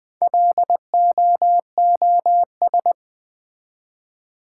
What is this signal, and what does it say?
Morse code, heard as a keyed tone, about 20 wpm, 690 Hz. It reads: LOOS
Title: Build A Word - 1 Letter At A Time - 5-Letter Words - 20wpm